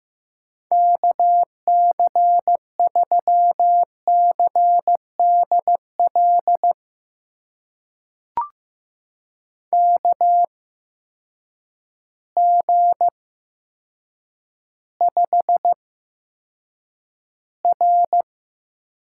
Morse code, heard as a keyed tone, about 15 wpm, spaced as for 7 wpm, 700 Hz. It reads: KC3CDL KG5R
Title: USA Call Signs - 6 Characters - 8x Character Spacing - 15wpm